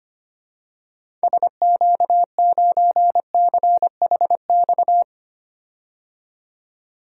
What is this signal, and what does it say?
Morse code, heard as a keyed tone, about 25 wpm, 700 Hz. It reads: SQ9CHX
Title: DXCC-100 Call Signs - 6 Characters - 25wpm